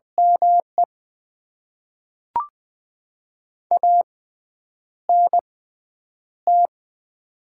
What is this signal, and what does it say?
Morse code, heard as a keyed tone, about 20 wpm, spaced as for 9 wpm, 700 Hz. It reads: ME ANT
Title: Top 20 QSO Elements - 6x Character Spacing - 20wpm